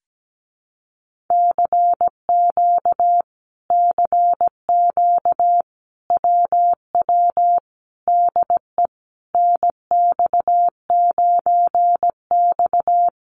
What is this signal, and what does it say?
Morse code, heard as a keyed tone, about 17 wpm, 700 Hz. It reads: CQ CQ WW DE NX9X
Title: Common QSO Phrases Encore 17wpm